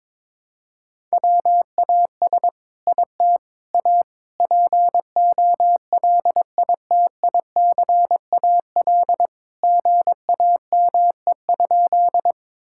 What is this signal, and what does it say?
Morse code, heard as a keyed tone, about 22 wpm, 700 Hz. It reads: WAS IT A POLITICAL GAME?
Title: Sentences from Top 1000 Words - Review Part 5 - 22wpm